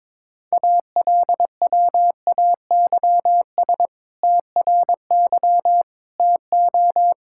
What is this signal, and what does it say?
Morse code, heard as a keyed tone, about 21 wpm, 700 Hz. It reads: ALWAYS TRY TO
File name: practice / 22wpm / Sentences with 2-Word Phrases - 22wpm